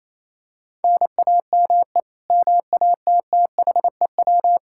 Morse code, heard as a keyed tone, about 27 wpm, 700 Hz. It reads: NAME MATTHEW
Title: K1USN SST QSO Practice 28wpm